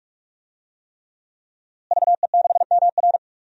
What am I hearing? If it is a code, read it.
Morse code, heard as a keyed tone, about 46 wpm, 700 Hz. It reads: VE6MR
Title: DXCC-100 Call Signs - 5 Characters - Repeated 3x - 45wpm